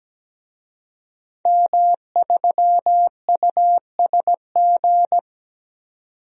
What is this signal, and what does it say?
Morse code, heard as a keyed tone, about 17 wpm, 700 Hz. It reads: M3USG
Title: DXCC-100 Call Signs - 5 Characters - 17wpm